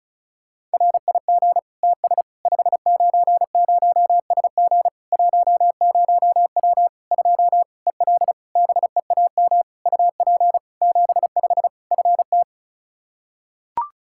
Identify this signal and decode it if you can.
Morse code, heard as a keyed tone, about 36 wpm, 700 Hz. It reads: RIG TS 590SG 10W 2 EL BEAM UP 75 FT
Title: Practice QSO - Exchange 2 - Info - 35wpm